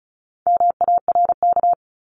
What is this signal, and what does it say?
Morse code, heard as a keyed tone, about 34 wpm, 700 Hz. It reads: MARK
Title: Top 500 Words Repeated 35wpm